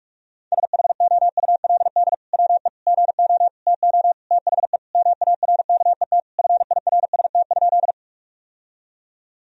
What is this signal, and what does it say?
Morse code, heard as a keyed tone, about 47 wpm, 700 Hz. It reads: SHOULD WE GO TO THE MARKET FIRST?
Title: Sentences from Top 1000 Words - Review Part 3 - 45wpm